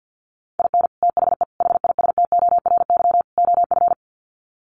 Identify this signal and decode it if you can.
Morse code, heard as a keyed tone, about 50 wpm, 700 Hz. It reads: IN THE HISTORY OF
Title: Sets of 4 Words 50wpm